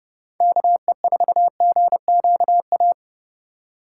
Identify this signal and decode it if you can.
Morse code, heard as a keyed tone, about 36 wpm, 700 Hz. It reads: KE4GQA
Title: Speed Racing - USA 2x3 Call Signs 30wpm